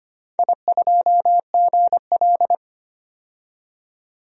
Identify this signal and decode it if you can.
Morse code, heard as a keyed tone, about 25 wpm, 700 Hz. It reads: I2GL
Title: DXCC-100 Call Signs - 4 Characters - 25wpm